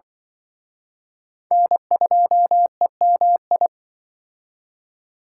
Morse code, heard as a keyed tone, about 24 wpm, 700 Hz. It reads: N2EMI
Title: Speed Racing - USA Call Signs 20wpm